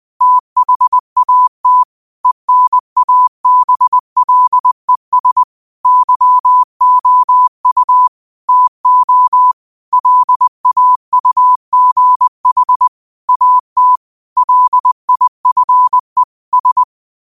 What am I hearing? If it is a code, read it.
Morse code, heard as a keyed tone, about 20 wpm, 1000 Hz. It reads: THAT ENABLES YOU TO LAUGH AT LIFE S
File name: qotdfiles / 20wpm